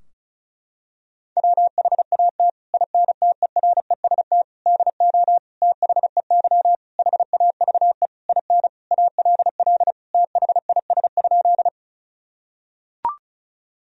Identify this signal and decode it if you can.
Morse code, heard as a keyed tone, about 34 wpm, 700 Hz. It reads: WHAT INTEREST DO THEY HAVE IN ALL THIS?
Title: Sentences from Top 1000 Words - Review Part 2 - 35wpm